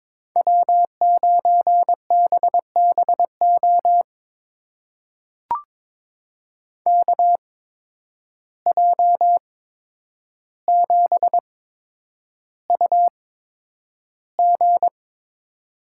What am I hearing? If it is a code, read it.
Morse code, heard as a keyed tone, about 21 wpm, spaced as for 9 wpm, 700 Hz. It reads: W9BBO KJ7UG
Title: USA Call Signs - 5 Characters - 8x Character Spacing - 22wpm